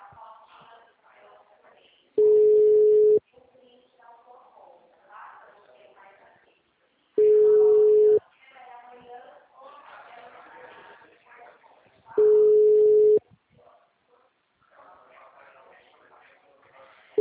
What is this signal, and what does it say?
Ringback tone (Europe, 425 Hz), with speech.